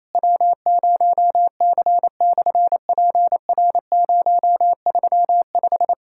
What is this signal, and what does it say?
Morse code, heard as a keyed tone, about 28 wpm, 700 Hz. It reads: W0C/PR035
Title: SOTA QSO Practice 28wpm